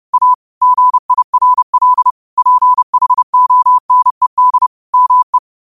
Morse code, heard as a keyed tone, about 30 wpm, 1000 Hz. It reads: A GIRL PHONED ME
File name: qotdfiles / 30wpm